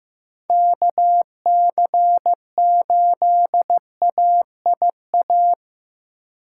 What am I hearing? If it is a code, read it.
Morse code, heard as a keyed tone, about 15 wpm, 700 Hz. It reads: KC8AIA
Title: USA Call Signs - 4x Character Spacing - 15wpm